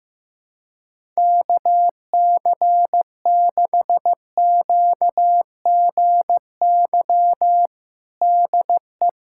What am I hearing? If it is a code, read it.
Morse code, heard as a keyed tone, about 16 wpm, 700 Hz. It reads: KC6QGY DE
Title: USA Callsign DE Callsign - 6 Characters - 15wpm